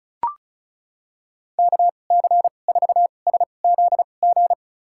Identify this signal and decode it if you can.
Morse code, heard as a keyed tone, about 36 wpm, spaced as for 25 wpm, 700 Hz. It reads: KC4SZG
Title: USA Call Signs - 6 Characters - 2x Character Spacing - 35wpm